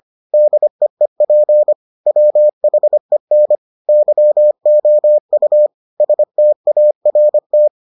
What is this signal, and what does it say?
Morse code, heard as a keyed tone, about 25 wpm, 600 Hz. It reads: DEEP WHEN YOU START